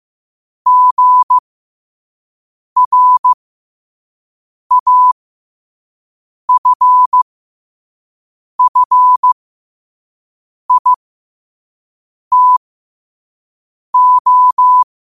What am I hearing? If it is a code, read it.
Morse code, heard as a keyed tone, about 15 wpm, spaced as for 5 wpm, 1000 Hz. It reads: GRAFFITO
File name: qotdfiles / 5wpm